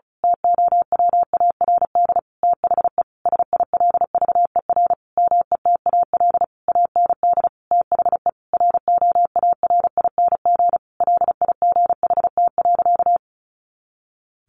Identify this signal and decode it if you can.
Morse code, heard as a keyed tone, about 34 wpm, 700 Hz. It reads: TOWARD THE SILVER METAL AND THE ROARING LIGHT.
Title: A Sound Of Thunder 35wpm